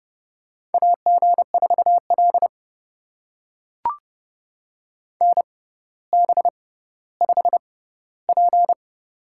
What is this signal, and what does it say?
Morse code, heard as a keyed tone, about 30 wpm, spaced as for 15 wpm, 690 Hz. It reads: AG4L NB5P
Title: USA Call Signs - 4 Characters - 6x Character Spacing - 30wpm